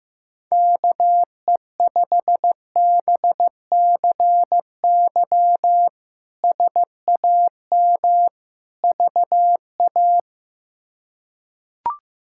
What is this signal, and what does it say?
Morse code, heard as a keyed tone, about 15 wpm, 700 Hz. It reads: KE5BCY SAM VA
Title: K1USN SST QSO Practice - 15wpm